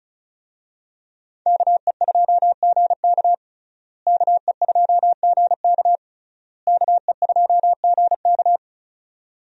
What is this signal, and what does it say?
Morse code, heard as a keyed tone, about 36 wpm, 700 Hz. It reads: KE2GK KE2GK KE2GK
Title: USA Call Signs - 5 Characters - Repeated 3x - 35wpm